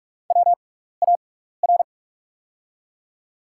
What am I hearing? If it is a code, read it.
Morse code, heard as a keyed tone, about 44 wpm, spaced as for 15 wpm, 700 Hz. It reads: WAR
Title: 6x Character Spacing - Top 500 Words - 45wpm